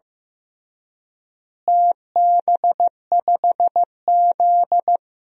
Morse code, heard as a keyed tone, about 15 wpm, 700 Hz. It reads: TB5Z
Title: DXCC-100 Call Signs - 4 Characters - Repeated 2x - 15wpm